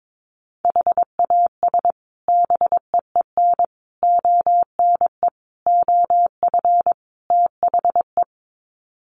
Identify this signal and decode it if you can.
Morse code, heard as a keyed tone, about 22 wpm, 700 Hz. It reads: HAS BEEN ONE OF THE
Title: Sets of 5 Words Encore 22wpm